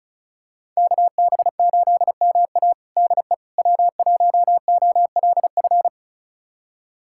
Morse code, heard as a keyed tone, about 34 wpm, 700 Hz. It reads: KB8MA DE W1OLF
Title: USA Callsign DE Callsign - 5 Characters - 35wpm